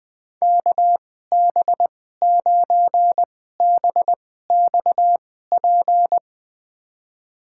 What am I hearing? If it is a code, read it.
Morse code, heard as a keyed tone, about 20 wpm, spaced as for 14 wpm, 700 Hz. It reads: KB9BXP
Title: USA Call Signs - 6 Characters - 2x Character Spacing - 20wpm